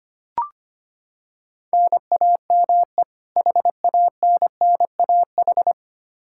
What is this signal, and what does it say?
Morse code, heard as a keyed tone, about 25 wpm, 700 Hz. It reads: NAME HANNAH
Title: Practice QSO - Exchange 1 - Beginning - 25wpm